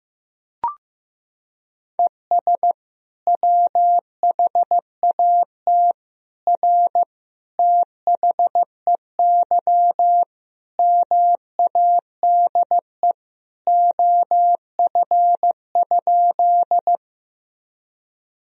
Morse code, heard as a keyed tone, about 15 wpm, 700 Hz. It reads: ES WHAT R THEY MADE OF?
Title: Sentences from Top 200 Words and Abbreviations 15wpm